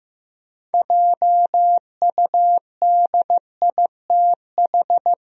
Morse code, heard as a keyed tone, about 15 wpm, 700 Hz. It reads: JUDITH
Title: USA Common Names - 15wpm